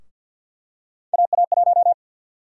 Morse code, heard as a keyed tone, about 53 wpm, 700 Hz. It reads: AA1
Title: USA Call Sign Prefixes - 50wpm